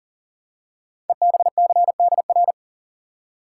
Morse code, heard as a keyed tone, about 39 wpm, 690 Hz. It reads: EBCDR
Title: Code Groups - 5 Letters - 40wpm